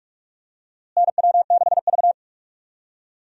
Morse code, heard as a keyed tone, about 42 wpm, 700 Hz. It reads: NW6V
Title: USA Call Signs - 4 Characters - Repeated 3x - 45wpm